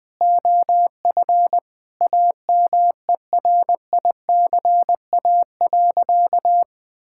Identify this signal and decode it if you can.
Morse code, heard as a keyed tone, about 21 wpm, 700 Hz. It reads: OF AMERICA.
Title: Sentences with 5-Word Phrases - Spoken Beginning - 20wpm